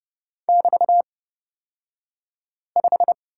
Morse code, heard as a keyed tone, about 30 wpm, spaced as for 5 wpm, 700 Hz. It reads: <BT>5
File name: Lesson 36d - BT - Two Characters 30wpm